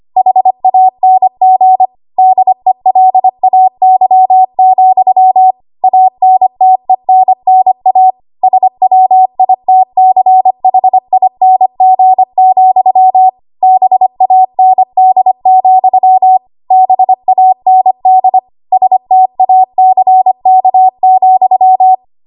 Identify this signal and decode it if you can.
Morse code, heard as a keyed tone, about 25 wpm, 750 Hz. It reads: HANG DELAY, ANTENNA SWITCHING, BAND, BAND STACK,
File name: Archive / 25 WPM / 25WPM